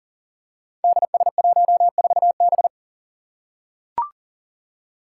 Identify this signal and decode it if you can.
Morse code, heard as a keyed tone, about 39 wpm, 700 Hz. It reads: DS14B
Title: Five Challenging Letter-Number Combinations 40wpm